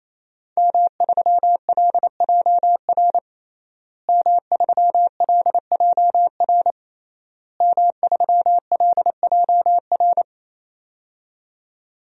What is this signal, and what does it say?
Morse code, heard as a keyed tone, about 28 wpm, 700 Hz. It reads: M3LJR M3LJR M3LJR
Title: DXCC-100 Call Signs - 5 Characters - Repeated 3x - 28wpm